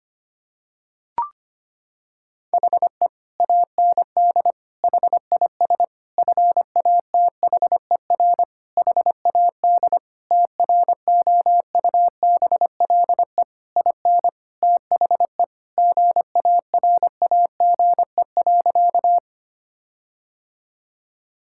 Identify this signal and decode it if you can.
Morse code, heard as a keyed tone, about 25 wpm, 700 Hz. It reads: HE AND HIS FATHER HAD TROUBLE IN THE GARAGE.
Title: Sentences from Top 1600 Words 25wpm